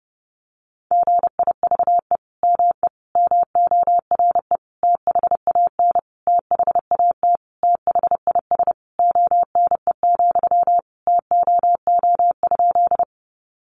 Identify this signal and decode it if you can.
Morse code, heard as a keyed tone, about 28 wpm, 700 Hz. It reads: GIVE ME MORE THAN THAT THIS ONE, TOO?
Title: Sentences from Top 200 Words 30wpm